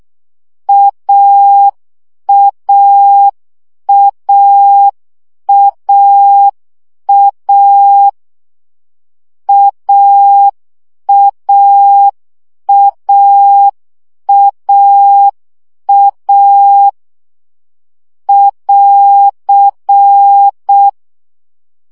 Morse code, heard as a keyed tone, about 6 wpm, 800 Hz. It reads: AAAAA AAAAA <AR>
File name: CW gyakorlas a 5 baud